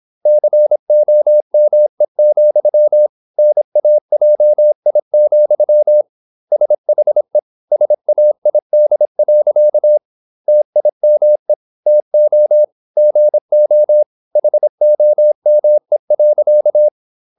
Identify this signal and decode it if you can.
Morse code, heard as a keyed tone, about 26 wpm, 600 Hz. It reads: COME, NAJI, SHE SAID. TIME TO GO HOME.
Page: Never 26wpm | CW med Gnister